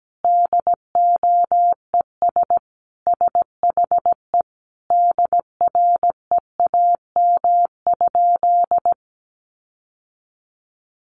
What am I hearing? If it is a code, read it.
Morse code, heard as a keyed tone, about 17 wpm, 700 Hz. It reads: DOES SHE DREAM?
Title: Sentences from Top 1500 Words 17wpm